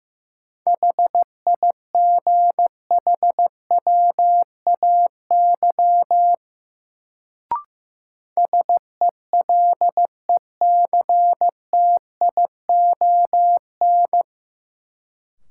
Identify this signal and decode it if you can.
Morse code, heard as a keyed tone, about 15 wpm, 700 Hz. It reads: HIGHWAY SELECTION
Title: Sentences from Top 1400 Words - New Word Drill - 15wpm